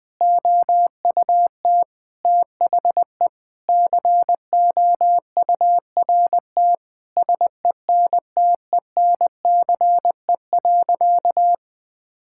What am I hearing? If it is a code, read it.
Morse code, heard as a keyed tone, about 20 wpm, 700 Hz. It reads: OUT THE COURT SENTENCE.
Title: Sentences from Top 1600 Words 20wpm